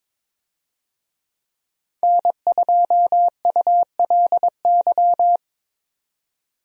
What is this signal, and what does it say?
Morse code, heard as a keyed tone, about 22 wpm, 700 Hz. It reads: N2ULY
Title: USA Call Signs - 5 Characters - 22wpm